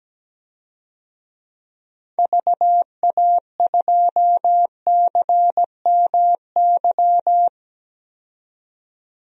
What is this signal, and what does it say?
Morse code, heard as a keyed tone, about 17 wpm, 700 Hz. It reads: VA2CMY
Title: DXCC-100 Call Signs - 6 Characters - 17wpm